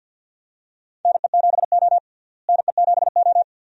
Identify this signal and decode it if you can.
Morse code, heard as a keyed tone, about 46 wpm, 700 Hz. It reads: NE7O NE7O
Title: USA Call Signs - 4 Characters - Repeated 2x - 50wpm